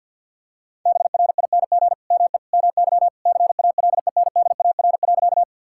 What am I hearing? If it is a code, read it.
Morse code, heard as a keyed tone, about 50 wpm, 700 Hz. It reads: BRING ME MY CALENDAR.
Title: Sentences from Top 1600 Words 50wpm